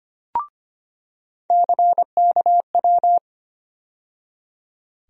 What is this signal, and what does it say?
Morse code, heard as a keyed tone, about 25 wpm, 700 Hz. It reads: CKW
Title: USA Call Sign Postfixes - 3 Letters 25wpm